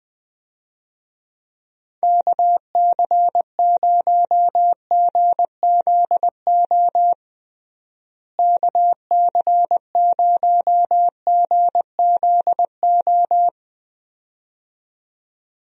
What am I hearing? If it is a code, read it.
Morse code, heard as a keyed tone, about 20 wpm, 700 Hz. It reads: KC0GZO KC0GZO
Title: USA Call Signs - 6 Characters - Repeated 2x - 20wpm